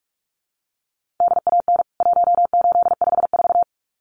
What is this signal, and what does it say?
Morse code, heard as a keyed tone, about 44 wpm, 700 Hz. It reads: DAN 1854
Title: CWT Exchange 45wpm